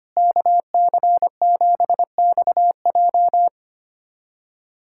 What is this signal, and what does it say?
Morse code, heard as a keyed tone, about 25 wpm, 700 Hz. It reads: KC7XJ
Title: USA Callsign DE Callsign - 5 Characters - 25wpm